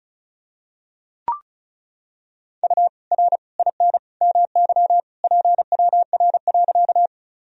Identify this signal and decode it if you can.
Morse code, heard as a keyed tone, about 36 wpm, 700 Hz. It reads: U R IN MY PWR.
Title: Sentences from Top 500 Words and Abbreviations 35wpm